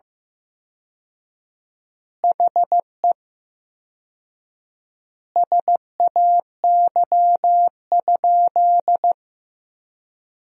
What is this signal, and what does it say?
Morse code, heard as a keyed tone, about 15 wpm, 700 Hz. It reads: HE SAY?
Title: 4x Word-Spacing - Sentences from Top 300 Words - 15wpm